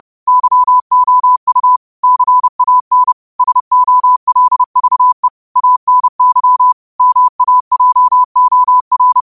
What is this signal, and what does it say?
Morse code, heard as a keyed tone, about 30 wpm, 1000 Hz. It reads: YOU CAN SOLVE ANY MAJOR
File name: qotdfiles / 30wpm